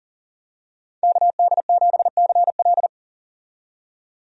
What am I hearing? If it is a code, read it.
Morse code, heard as a keyed tone, about 38 wpm, 700 Hz. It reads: KD7CL
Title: USA Call Signs - 5 Characters - 40wpm